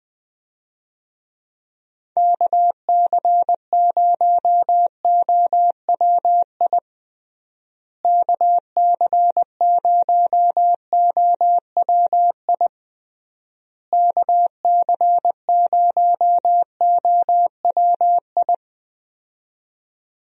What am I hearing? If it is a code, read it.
Morse code, heard as a keyed tone, about 20 wpm, 700 Hz. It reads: KC0OWI KC0OWI KC0OWI
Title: USA Call Signs - 6 Characters - Repeated 3x - 20wpm